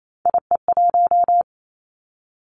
Morse code, heard as a keyed tone, about 28 wpm, 700 Hz. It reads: IE1
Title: Three Challenging Letter-Number Combinations 28wpm